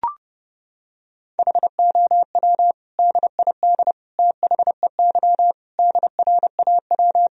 Morse code, heard as a keyed tone, about 30 wpm, 700 Hz. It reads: HOW DID THEY DRAW
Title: Sentences from Top 1000 Words - Review Part 1 - 30wpm